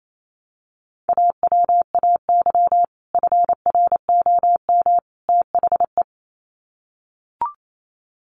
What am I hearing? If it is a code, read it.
Morse code, heard as a keyed tone, about 28 wpm, 700 Hz. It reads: AWAY FROM THE
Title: Sets of 3 Words 28wpm